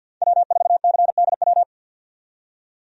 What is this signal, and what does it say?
Morse code, heard as a keyed tone, about 53 wpm, 700 Hz. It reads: W4CDW
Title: USA Call Signs - 5 Characters - 50wpm